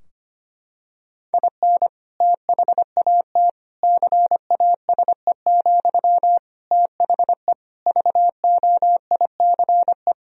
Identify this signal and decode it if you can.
Morse code, heard as a keyed tone, about 25 wpm, 700 Hz. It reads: IN THAT CASE, THE VOICE
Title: Sentences from Top 1000 Words - Review Part 2 - 25wpm